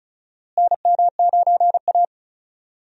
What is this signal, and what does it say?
Morse code, heard as a keyed tone, about 34 wpm, 700 Hz. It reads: NM9A
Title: USA Call Signs - 4 Characters - Repeated 3x - 35wpm